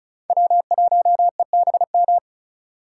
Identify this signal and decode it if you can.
Morse code, heard as a keyed tone, about 36 wpm, 700 Hz. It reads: W1EBM
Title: K1USN SST QSO Practice - 35wpm